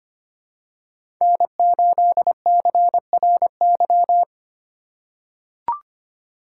Morse code, heard as a keyed tone, about 25 wpm, 700 Hz. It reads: N8CRY
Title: USA Call Signs - 5 Characters - 4x Character Spacing - 25wpm